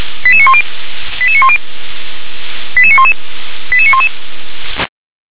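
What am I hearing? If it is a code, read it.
Five-tone selective call (ZVEI), heard as quick runs of five tones, tones of about 70 ms. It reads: 88910 88910 88911 88911
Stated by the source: ZVEI five-tone selective call received via SDR.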